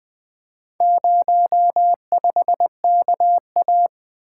Morse code, heard as a keyed tone, about 20 wpm, 700 Hz. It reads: 05KA
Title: Four Letter-Number Combinations - 20wpm